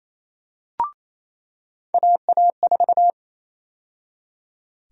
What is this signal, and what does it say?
Morse code, heard as a keyed tone, about 28 wpm, 700 Hz. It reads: AA4
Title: USA Call Sign Prefixes - 28wpm